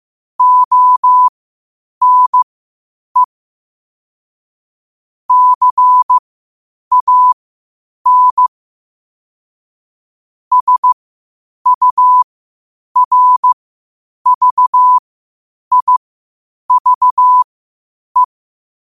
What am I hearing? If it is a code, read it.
Morse code, heard as a keyed tone, about 15 wpm, spaced as for 8 wpm, 1000 Hz. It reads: ONE CAN SURVIVE
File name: 8wpm